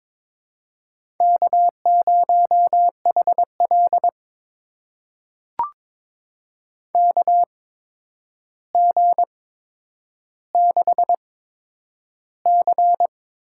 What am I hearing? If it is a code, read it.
Morse code, heard as a keyed tone, about 22 wpm, spaced as for 9 wpm, 700 Hz. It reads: K0HL KG6C
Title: USA Call Signs - 8x Character Spacing - 22wpm